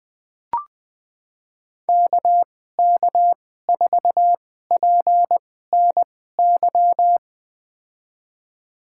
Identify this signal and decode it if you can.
Morse code, heard as a keyed tone, about 20 wpm, spaced as for 14 wpm, 700 Hz. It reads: KK4PNY
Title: USA Call Signs - 6 Characters - 2x Character Spacing - 20wpm